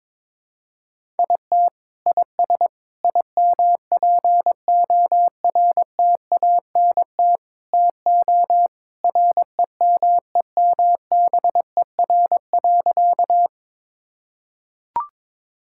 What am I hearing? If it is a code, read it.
Morse code, heard as a keyed tone, about 22 wpm, 700 Hz. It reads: IT IS IMPORTANT TO REMEMBER.
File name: practice / 22wpm / Sentences with 4-Word Phrases - 22wpm